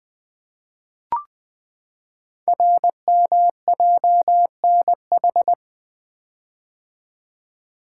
Morse code, heard as a keyed tone, about 20 wpm, 700 Hz. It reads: RMJNH
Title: Code Groups - 5 Letters - 20wpm